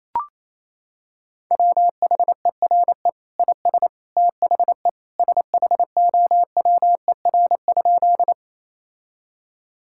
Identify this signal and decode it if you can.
Morse code, heard as a keyed tone, about 28 wpm, 700 Hz. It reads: WHERE IS THE SHOWER?
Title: Sentences from Top 1600 Words 28wpm